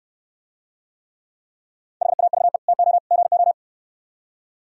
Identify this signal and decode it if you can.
Morse code, heard as a keyed tone, about 67 wpm, 700 Hz. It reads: HAVE TO GO